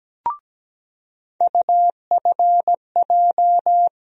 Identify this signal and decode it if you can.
Morse code, heard as a keyed tone, about 17 wpm, 700 Hz. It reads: UFJ
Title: USA Call Sign Postfixes - 3 Letters 17wpm